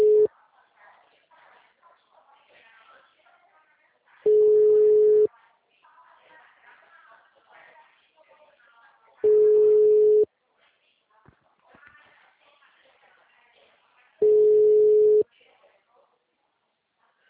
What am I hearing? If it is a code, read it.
Ringback tone (Europe, 425 Hz), with speech.
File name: out- -2002- - - .173